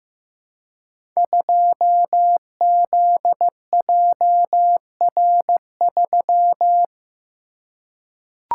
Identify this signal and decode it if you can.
Morse code, heard as a keyed tone, about 15 wpm, 700 Hz. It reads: 2ZJR3
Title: Code Groups - 5 Letter:Numbers - 15wpm